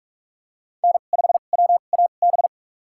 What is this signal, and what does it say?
Morse code, heard as a keyed tone, about 46 wpm, spaced as for 33 wpm, 700 Hz. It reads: N5WAB
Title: USA Call Signs - 5 Characters - 2x Character Spacing - 45wpm